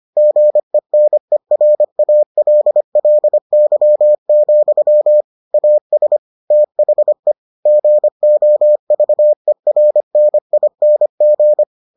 Morse code, heard as a keyed tone, about 25 wpm, 600 Hz. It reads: GENERALLY, AS THE GOVERNING